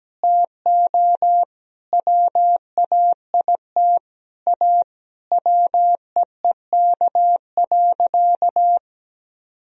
Morse code, heard as a keyed tone, about 17 wpm, 700 Hz. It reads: TO WAIT A WEEK.
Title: Sentences from Top 1000 Words - Review Part 2 - 17wpm